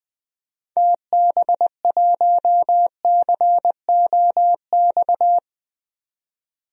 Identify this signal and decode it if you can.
Morse code, heard as a keyed tone, about 20 wpm, 700 Hz. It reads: TB1COX
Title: DXCC-100 Call Signs - 6 Characters - 20wpm